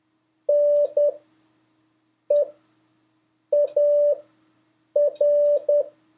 Morse code, heard as a keyed tone, about 10 wpm, spaced as for 6 wpm, 580 Hz. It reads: NEAR
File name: 2of5-words-1